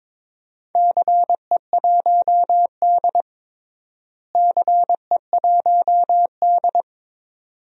Morse code, heard as a keyed tone, about 22 wpm, 700 Hz. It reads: CE1D CE1D
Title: DXCC-100 Call Signs - 4 Characters - Repeated 2x - 22wpm